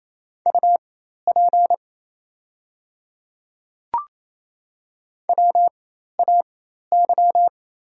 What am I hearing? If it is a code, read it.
Morse code, heard as a keyed tone, about 28 wpm, spaced as for 13 wpm, 690 Hz. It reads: UP WAY
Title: 4x Character Spacing - Top 300 Words - 28wpm